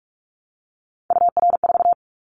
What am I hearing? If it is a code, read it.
Morse code, heard as a keyed tone, about 44 wpm, 700 Hz. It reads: UR4
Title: Callsign Prefixes for Top 25 DXCC - 45wpm